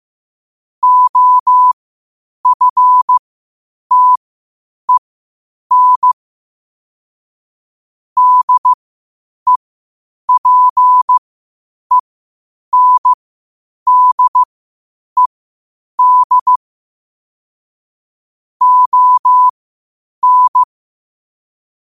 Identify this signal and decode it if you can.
Morse code, heard as a keyed tone, about 15 wpm, spaced as for 8 wpm, 1000 Hz. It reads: OFTEN DEPENDED ON
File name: qotdfiles / 8wpm